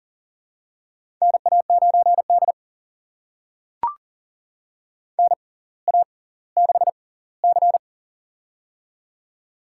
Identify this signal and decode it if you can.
Morse code, heard as a keyed tone, about 40 wpm, spaced as for 20 wpm, 700 Hz. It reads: NA9D NA6C
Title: USA Call Signs - 4 Characters - 6x Character Spacing - 40wpm